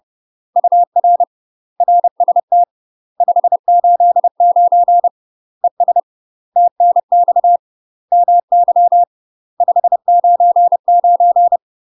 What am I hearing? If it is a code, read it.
Morse code, heard as a keyed tone, about 30 wpm, 700 Hz. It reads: UR RST 589 ES TNX MY 599
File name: UK MASTER QSOs TRAINING / 30WPM / UK MASTER QSOs 0020